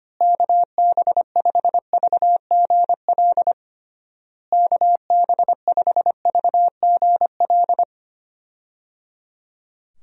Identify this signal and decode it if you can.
Morse code, heard as a keyed tone, about 25 wpm, 700 Hz. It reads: KB5VGL KB5VGL
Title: USA Call Signs - 6 Characters - Repeated 3x - 25wpm